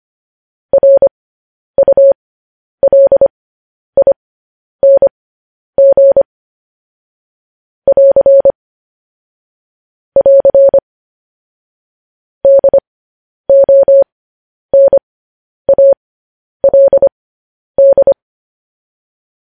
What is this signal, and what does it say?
Morse code, heard as a keyed tone, about 25 wpm, spaced as for 10 wpm, 550 Hz. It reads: RULING <AR> <AR> DONALD
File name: podcast / News headlines-1x-25x10